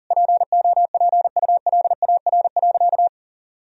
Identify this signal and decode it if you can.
Morse code, heard as a keyed tone, about 40 wpm, 700 Hz. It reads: POPULAR.
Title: Sentences from Top 1000 Words - Review Part 3 - 40wpm